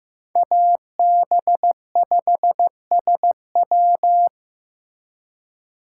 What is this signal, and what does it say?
Morse code, heard as a keyed tone, about 18 wpm, 700 Hz. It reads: AB5SW
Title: Speed Racing - USA 2x2 Call Signs 15wpm